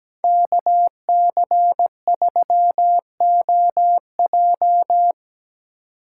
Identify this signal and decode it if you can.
Morse code, heard as a keyed tone, about 19 wpm, 700 Hz. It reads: KC3OJ
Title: Speed Racing - USA 2x2 Call Signs 17wpm